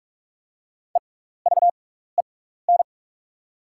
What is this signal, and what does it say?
Morse code, heard as a keyed tone, about 44 wpm, spaced as for 15 wpm, 700 Hz. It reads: EVEN
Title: 6x Character Spacing - Top 300 Words - 45wpm